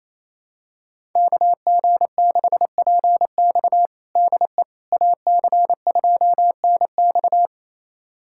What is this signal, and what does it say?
Morse code, heard as a keyed tone, about 28 wpm, 700 Hz. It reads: KG6PX DE AC2NX
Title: USA Callsign DE Callsign - 5 Characters - 28wpm